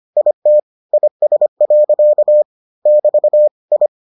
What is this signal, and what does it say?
Morse code, heard as a keyed tone, about 25 wpm, 600 Hz. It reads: IT IS. <BT> I